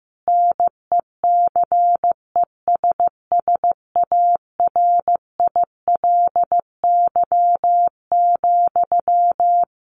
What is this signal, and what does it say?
Morse code, heard as a keyed tone, about 15 wpm, 700 Hz. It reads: NECESSARILY,
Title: Sentences from Top 1400 Words 15wpm